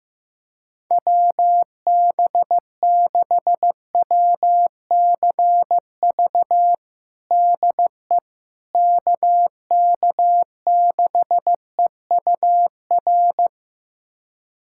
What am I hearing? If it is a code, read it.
Morse code, heard as a keyed tone, about 15 wpm, 700 Hz. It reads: WB6WCV DE KK6EUR
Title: USA Callsign DE Callsign - 6 Characters - 15wpm